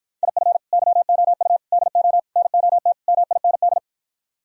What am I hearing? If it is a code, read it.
Morse code, heard as a keyed tone, about 52 wpm, 700 Hz. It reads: IF YOU DO NOT MIND